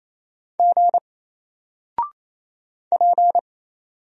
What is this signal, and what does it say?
Morse code, heard as a keyed tone, about 27 wpm, 700 Hz. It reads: G P
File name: Single-Letter-Number-28wpm